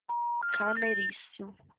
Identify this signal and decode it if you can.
Special information tone, with speech.